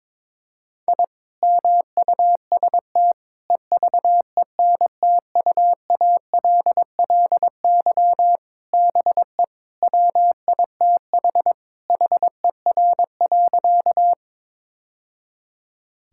Morse code, heard as a keyed tone, about 22 wpm, 700 Hz. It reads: I MUST EVENTUALLY BE WITH HER.
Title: Sentences from Top 1000 Words - Review Part 6 - 22wpm